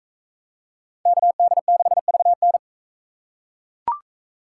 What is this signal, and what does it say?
Morse code, heard as a keyed tone, about 42 wpm, 700 Hz. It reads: KD6VN